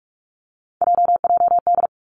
Morse code, heard as a keyed tone, about 46 wpm, 700 Hz. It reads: JJD
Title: Three Challenging Letter-Number Combinations 45wpm